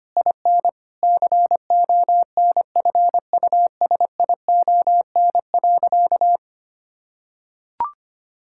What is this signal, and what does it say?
Morse code, heard as a keyed tone, about 25 wpm, 700 Hz. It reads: IN CONFUSION.
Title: Sentences from Top 1600 Words 25wpm